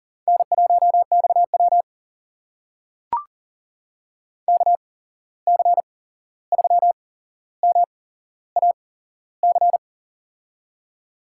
Morse code, heard as a keyed tone, about 40 wpm, spaced as for 15 wpm, 700 Hz. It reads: N1XW KC3MAC
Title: USA Call Signs - 8x Character Spacing - 40wpm